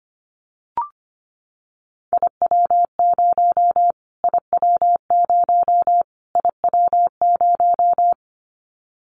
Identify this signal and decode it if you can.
Morse code, heard as a keyed tone, about 25 wpm, 700 Hz. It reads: IW0 IW0 IW0
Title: Callsign Prefixes for Top 25 DXCC - Repeated 3x - 25wpm